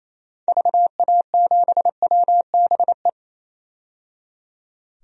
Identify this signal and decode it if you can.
Morse code, heard as a keyed tone, about 28 wpm, 700 Hz. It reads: VA7WBE
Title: DXCC-100 Call Signs - 6 Characters - 28wpm